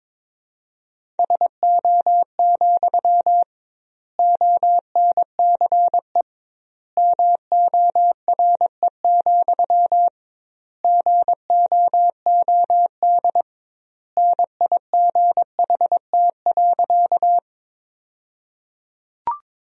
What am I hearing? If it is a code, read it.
Morse code, heard as a keyed tone, about 22 wpm, 700 Hz. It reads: SO, ONCE MORE, GOOD NIGHT.
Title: 2x Word-Spacing - Sentences from Top 300 Words - 22wpm